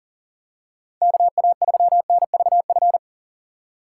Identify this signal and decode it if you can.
Morse code, heard as a keyed tone, about 40 wpm, 700 Hz. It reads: KA3NVF
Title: USA Call Signs - 6x Character Spacing - 40wpm